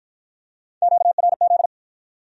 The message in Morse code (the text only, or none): QRZ